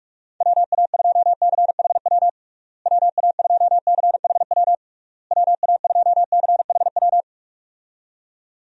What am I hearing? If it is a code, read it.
Morse code, heard as a keyed tone, about 46 wpm, 700 Hz. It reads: WA2CHW WA2CHW WA2CHW
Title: USA Call Signs - 6 Characters - Repeated 3x - 45wpm